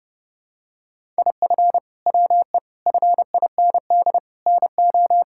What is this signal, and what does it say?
Morse code, heard as a keyed tone, about 32 wpm, 700 Hz. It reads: IF WE FIND NO
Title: Sentences from Top 1600 Words 30wpm